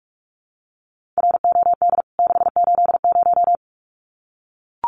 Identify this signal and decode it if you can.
Morse code, heard as a keyed tone, about 44 wpm, 700 Hz. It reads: ROD 680
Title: CWT Exchange 45wpm